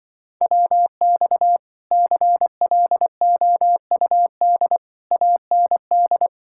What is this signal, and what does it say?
Morse code, heard as a keyed tone, about 24 wpm, 700 Hz. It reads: WX CLOUD AND
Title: Speed-Racing Common QSO Phrases Encore - 20wpm